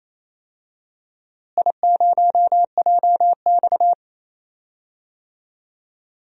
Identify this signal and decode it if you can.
Morse code, heard as a keyed tone, about 28 wpm, 700 Hz. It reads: I0JX
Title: DXCC-100 Call Signs - 4 Characters - Repeated 3x - 28wpm